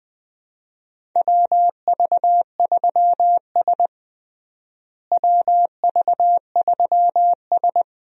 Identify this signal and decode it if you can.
Morse code, heard as a keyed tone, about 20 wpm, 700 Hz. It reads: WV3S WV3S
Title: USA Call Signs - 4 Characters - Repeated 3x - 20wpm